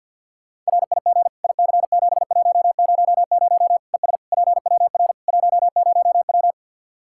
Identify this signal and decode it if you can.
Morse code, heard as a keyed tone, about 52 wpm, 700 Hz. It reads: RIG IC7100 ES PWR 10W
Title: K1USN SST QSO Practice 50wpm